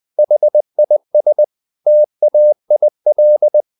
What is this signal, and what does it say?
Morse code, heard as a keyed tone, about 20 wpm, 600 Hz. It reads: HIS TAIL
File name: the old man and the sea / the old man and the sea part 1 20wpm